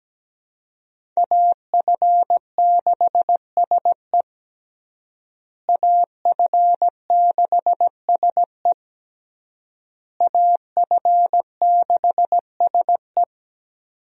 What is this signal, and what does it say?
Morse code, heard as a keyed tone, about 17 wpm, 700 Hz. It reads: AF6SE AF6SE AF6SE
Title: USA Call Signs - 5 Characters - Repeated 3x - 17wpm